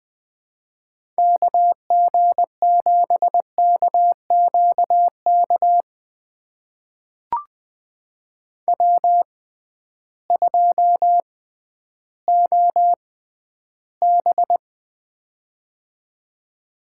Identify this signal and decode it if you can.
Morse code, heard as a keyed tone, about 20 wpm, spaced as for 12 wpm, 700 Hz. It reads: KG7KQK W2OB
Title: USA Call Signs - 6x Character Spacing - 20wpm